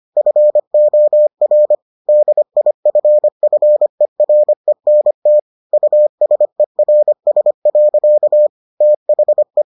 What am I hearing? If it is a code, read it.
Morse code, heard as a keyed tone, about 25 wpm, 600 Hz. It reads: FOR DIFFERENT USERS. THE